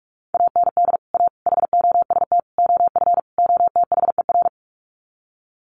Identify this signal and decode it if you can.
Morse code, heard as a keyed tone, about 44 wpm, 700 Hz. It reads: AND A HOST OF OTHER
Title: Sets of 5 Words Encore2 45wpm